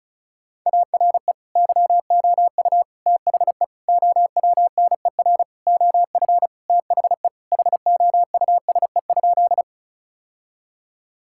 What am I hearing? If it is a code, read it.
Morse code, heard as a keyed tone, about 36 wpm, 700 Hz. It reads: ARE YOU THE OWNER OF THE HOUSE?
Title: Sentences from Top 1600 Words 35wpm